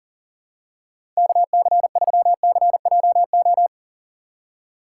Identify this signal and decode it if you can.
Morse code, heard as a keyed tone, about 40 wpm, 700 Hz. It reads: KC3CJO
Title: USA Call Signs - 2x Character Spacing - 40wpm